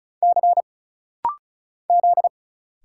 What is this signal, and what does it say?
Morse code, heard as a keyed tone, about 36 wpm, 700 Hz. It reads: C Z
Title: Single Letter-Number Alt - Rapid-Fire 35wpm